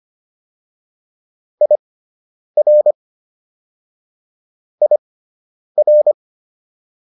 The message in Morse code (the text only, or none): IR IR